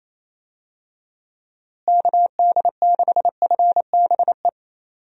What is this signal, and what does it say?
Morse code, heard as a keyed tone, about 28 wpm, 700 Hz. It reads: KD6FBE
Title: USA Call Signs - 6 Characters - Repeated 2x - 28wpm